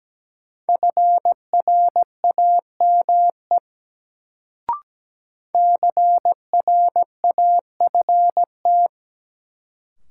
Morse code, heard as a keyed tone, about 17 wpm, 700 Hz. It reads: FRAME CRAFT
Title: Sentences from Top 1000 Words - Review Part 6 - New Word Drill - 17wpm